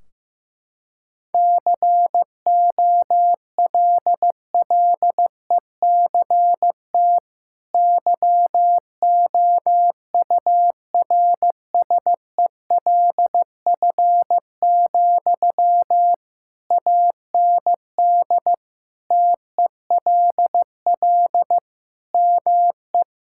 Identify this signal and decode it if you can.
Morse code, heard as a keyed tone, about 15 wpm, 700 Hz. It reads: COLLECT YOURSELF, AND TELL ME
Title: Sentences from Top 1600 Words 15wpm